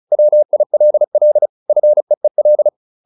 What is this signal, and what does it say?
Morse code, heard as a keyed tone, about 34 wpm, 600 Hz. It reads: WILL FEEL